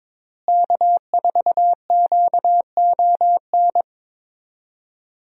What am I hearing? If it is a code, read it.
Morse code, heard as a keyed tone, about 22 wpm, 700 Hz. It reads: K4QON
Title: USA Call Signs - 5 Characters - Repeated 3x - 22wpm